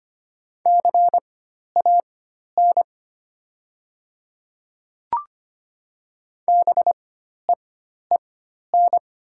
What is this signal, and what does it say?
Morse code, heard as a keyed tone, about 25 wpm, spaced as for 12 wpm, 690 Hz. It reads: CAN BEEN
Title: 4x Character Spacing - Top 100 Words - 25wpm